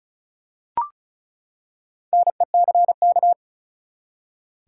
Morse code, heard as a keyed tone, about 36 wpm, 700 Hz. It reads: NECK
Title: Four Letter Words - 35wpm